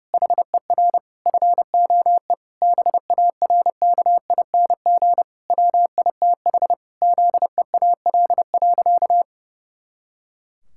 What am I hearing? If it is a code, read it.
Morse code, heard as a keyed tone, about 30 wpm, 700 Hz. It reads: HER FOE BARKING WITH ZEAL.
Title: Pangram Challenge v2 - 30wpm